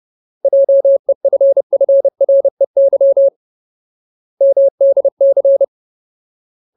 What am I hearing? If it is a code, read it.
Morse code, heard as a keyed tone, about 30 wpm, 550 Hz. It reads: JEFFREY MDC
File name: Ultimate Word List / 30WPM Ultimate